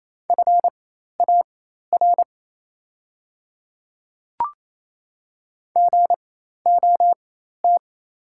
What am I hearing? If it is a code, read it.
Morse code, heard as a keyed tone, about 28 wpm, spaced as for 13 wpm, 700 Hz. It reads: FAR GOT
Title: 4x Character Spacing - Top 300 Words - 28wpm